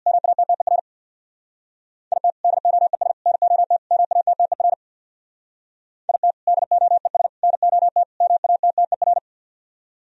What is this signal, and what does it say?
Morse code, heard as a keyed tone, about 64 wpm, 700 Hz. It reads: MATTER IT DOES NOT MATTER IT DOES NOT MATTER